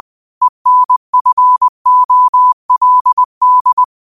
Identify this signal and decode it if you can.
Morse code, heard as a keyed tone, about 20 wpm, 1000 Hz. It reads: ENFOLD
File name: qotdfiles / 20wpm